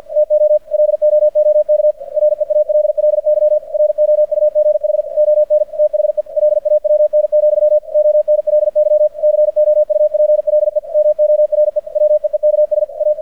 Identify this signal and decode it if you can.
Morse code, heard as a keyed tone, about 50 wpm, 600 Hz. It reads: TO LOOK REALLY ROUGH ON THE LAWN, ONLY COUPLE MORE WEEKS A
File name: archived / CRSnet